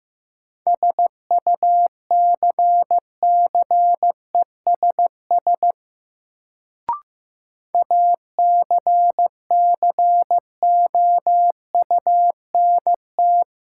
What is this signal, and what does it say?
Morse code, heard as a keyed tone, about 15 wpm, 700 Hz. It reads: SUCCESS ACCOUNT
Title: Sentences from Top 900 Words - New Word Drill - 15wpm